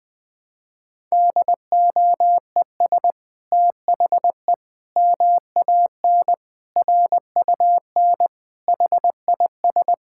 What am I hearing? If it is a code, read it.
Morse code, heard as a keyed tone, about 19 wpm, 700 Hz. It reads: DOES THE MAN RUN HIS
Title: Sentences from Top 1000 Words - Review Part 5 - 20wpm